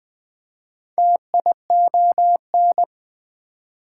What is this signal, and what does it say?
Morse code, heard as a keyed tone, about 20 wpm, 700 Hz. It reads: TION
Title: Prefixes and Suffixes 20wpm